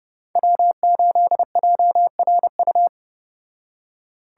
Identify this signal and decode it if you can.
Morse code, heard as a keyed tone, about 29 wpm, 700 Hz. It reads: W8JRU